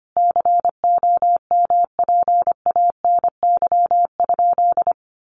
Morse code, heard as a keyed tone, about 25 wpm, 700 Hz. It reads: COMPANY?
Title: Sentences from Top 600 Words 25wpm